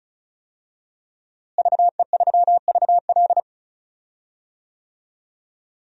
Morse code, heard as a keyed tone, about 34 wpm, 700 Hz. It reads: VE3VL
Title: DXCC-100 Call Signs - 5 Characters - Repeated 2x - 35wpm